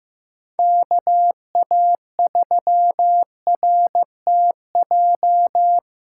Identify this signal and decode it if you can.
Morse code, heard as a keyed tone, about 16 wpm, 700 Hz. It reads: KA3RTJ
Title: USA Callsign DE Callsign 15wpm